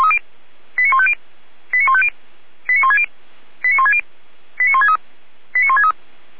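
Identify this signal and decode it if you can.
Five-tone selective call (ZVEI), heard as quick runs of five tones, tones of about 70 ms. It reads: 89150 89150 89160 89160 89163 89163
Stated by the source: ZVEI five-tone selective call received by SDR.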